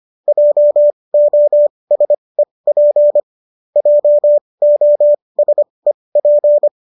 Morse code, heard as a keyed tone, about 25 wpm, spaced as for 20 wpm, 590 Hz. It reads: JOSEP JOSEP
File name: qso214 20